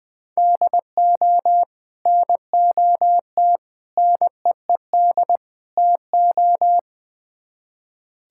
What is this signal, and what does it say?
Morse code, heard as a keyed tone, about 20 wpm, 700 Hz. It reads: DO NOT NEED TO
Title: Build A Phrase - 4-Word Phrases - 20wpm